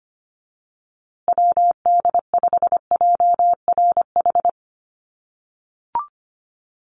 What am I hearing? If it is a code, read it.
Morse code, heard as a keyed tone, about 25 wpm, 700 Hz. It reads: WD5JRH
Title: USA Call Signs Repeated 2x 25wpm